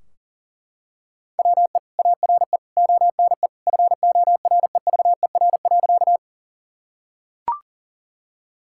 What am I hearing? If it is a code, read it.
Morse code, heard as a keyed tone, about 37 wpm, 700 Hz. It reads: WE ARE ONE FOREVER.
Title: Sentences from Top 1600 Words 40wpm